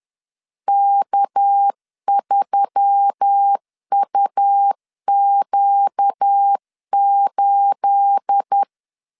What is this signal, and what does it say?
Morse code, heard as a keyed tone, about 10 wpm, 800 Hz. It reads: K3UQ8
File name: LEZIONI / 9 28 10wpm 2842